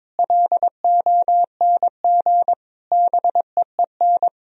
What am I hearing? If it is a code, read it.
Morse code, heard as a keyed tone, about 22 wpm, 700 Hz. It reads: LONG BEEN
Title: Sentences from Top 1000 Words - Review Part 3 - 22wpm